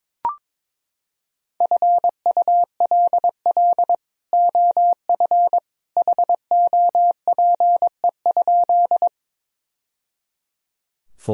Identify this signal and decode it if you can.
Morse code, heard as a keyed tone, about 22 wpm, 700 Hz. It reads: FULL OF HOPE?
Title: Sentences from Top 1000 Words - Review Part 3 - 22wpm